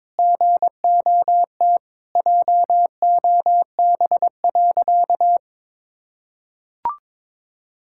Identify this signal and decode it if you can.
Morse code, heard as a keyed tone, about 22 wpm, 700 Hz. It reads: GOT JOB.
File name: practice / 22wpm / Sentences with 4-Word Phrases - 22wpm